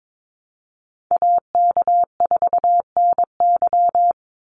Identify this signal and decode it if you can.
Morse code, heard as a keyed tone, about 22 wpm, 700 Hz. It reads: AK4NY
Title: Call Signs - 22wpm